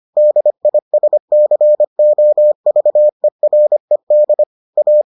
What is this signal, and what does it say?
Morse code, heard as a keyed tone, about 25 wpm, 600 Hz. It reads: DISCOVERED A